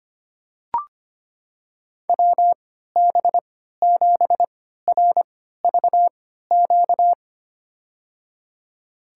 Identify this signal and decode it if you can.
Morse code, heard as a keyed tone, about 25 wpm, spaced as for 14 wpm, 700 Hz. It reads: WB7RVQ
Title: USA Call Signs - 6 Characters - 3x Character Spacing - 25wpm